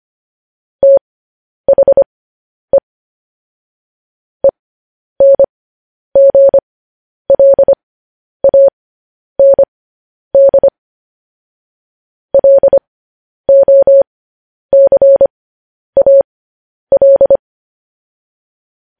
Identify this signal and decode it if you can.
Morse code, heard as a keyed tone, about 25 wpm, spaced as for 10 wpm, 550 Hz. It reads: THE ENGLAND LOCAL
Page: news-headlines-in-morse-code-at-10-wpm